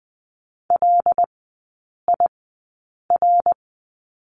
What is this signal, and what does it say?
Morse code, heard as a keyed tone, about 20 wpm, 700 Hz. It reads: L I R
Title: 133 Lesson 22e - Question Mark - Three Characters 20wpm